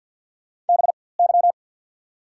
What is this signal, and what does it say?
Morse code, heard as a keyed tone, about 53 wpm, spaced as for 24 wpm, 690 Hz. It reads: BY
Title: 4x Character Spacing - Top 300 Words - 50wpm